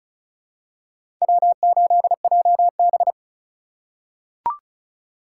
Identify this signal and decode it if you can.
Morse code, heard as a keyed tone, about 36 wpm, 700 Hz. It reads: W8JB
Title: USA Call Signs - 4 Characters - 2x Character Spacing - 35wpm